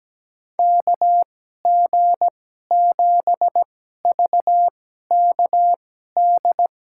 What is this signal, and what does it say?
Morse code, heard as a keyed tone, about 17 wpm, spaced as for 12 wpm, 700 Hz. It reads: KG7VKD
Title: USA Call Signs - 2x Character Spacing - 17wpm